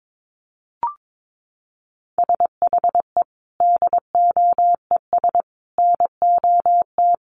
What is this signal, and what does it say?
Morse code, heard as a keyed tone, about 22 wpm, 700 Hz. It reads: SHE DOES NOT
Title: Easy Phrases 22wpm